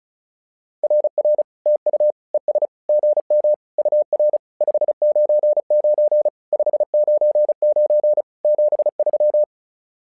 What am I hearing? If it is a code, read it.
Morse code, heard as a keyed tone, about 36 wpm, 600 Hz. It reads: RR TU ES GM UR 599 599 73
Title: SOTA QSO Practice 35wpm